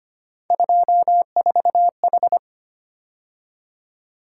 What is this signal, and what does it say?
Morse code, heard as a keyed tone, about 25 wpm, 700 Hz. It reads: 24H
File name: Three Letter-Number Combinations - 25wpm